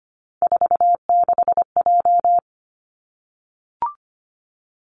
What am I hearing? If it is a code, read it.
Morse code, heard as a keyed tone, about 25 wpm, 700 Hz. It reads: 46J
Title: Three Challenging Letter-Number Combinations 25wpm